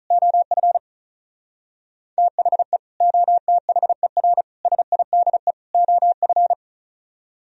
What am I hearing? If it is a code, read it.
Morse code, heard as a keyed tone, about 44 wpm, 700 Hz. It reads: OF THE OTHER SIDE OF